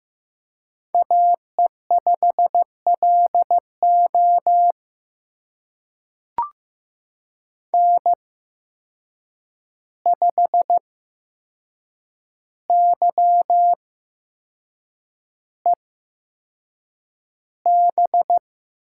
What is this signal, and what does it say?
Morse code, heard as a keyed tone, about 15 wpm, spaced as for 6 wpm, 700 Hz. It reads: AE5LO N5YEB
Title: USA Call Signs - 5 Characters - 8x Character Spacing - 15wpm